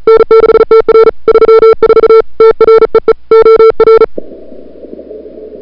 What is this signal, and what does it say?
Morse code, heard as a keyed tone, about 35 wpm, 440 Hz. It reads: N6TR 34 TREE OR
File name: n6tr / Fab2007CWSprintHour1